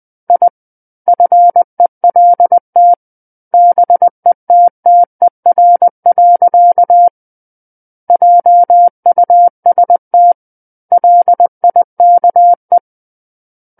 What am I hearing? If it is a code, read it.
Morse code, heard as a keyed tone, about 20 wpm, 700 Hz. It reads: I FELT BETTER. JUST LIKE
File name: 20 WPM